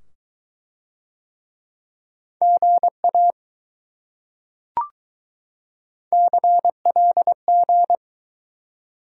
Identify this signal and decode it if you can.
Morse code, heard as a keyed tone, about 20 wpm, 700 Hz. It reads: GA CLG